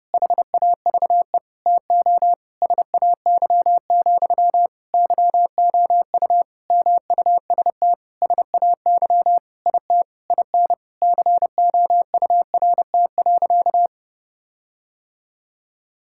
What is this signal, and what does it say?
Morse code, heard as a keyed tone, about 29 wpm, 700 Hz. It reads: HAVE TO SAY, YOU MUST SAY IT IN COURT.
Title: Sentences from Top 1600 Words 30wpm